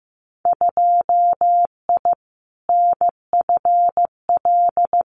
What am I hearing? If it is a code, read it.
Morse code, heard as a keyed tone, about 15 wpm, 700 Hz. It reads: 2I NFL
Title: Winter Field Day Exchanges 15wpm